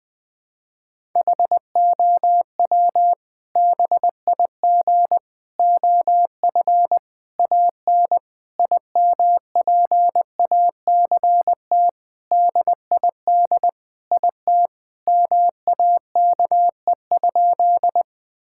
Morse code, heard as a keyed tone, about 20 wpm, 700 Hz. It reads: HOW BIG OF AN IMPACT DID IT MAKE?
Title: Sentences from Top 1000 Words - Review Part 6 - 20wpm